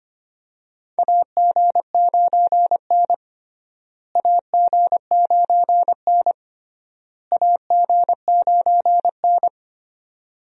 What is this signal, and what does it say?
Morse code, heard as a keyed tone, about 25 wpm, 700 Hz. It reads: AG9N AG9N AG9N
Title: USA Call Signs - 4 Characters - Repeated 3x - 25wpm